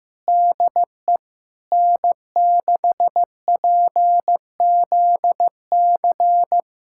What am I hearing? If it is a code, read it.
Morse code, heard as a keyed tone, about 16 wpm, 700 Hz. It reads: DE N6PZC
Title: USA Callsign DE Callsign 15wpm